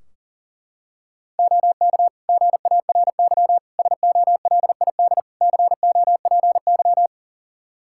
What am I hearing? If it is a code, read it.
Morse code, heard as a keyed tone, about 40 wpm, 700 Hz. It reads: OK GARY SOLID COPY
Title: Practice QSO - Exchange 3 - More Info - 40wpm